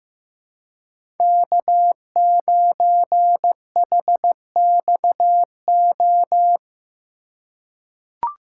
Morse code, heard as a keyed tone, about 15 wpm, 700 Hz. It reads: K9HXO
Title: USA Call Signs - 5 Characters - 3x Character Spacing - 15wpm